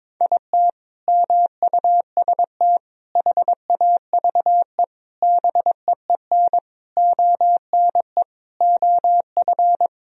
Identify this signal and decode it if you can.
Morse code, heard as a keyed tone, about 22 wpm, 700 Hz. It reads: IT MUST HAVE BEEN ONE OF
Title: Sentences from Top 1000 Words - Review Part 1 - 22wpm